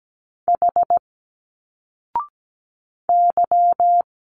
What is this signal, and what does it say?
Morse code, heard as a keyed tone, about 17 wpm, 700 Hz. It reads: H Y
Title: Single Letters 17wpm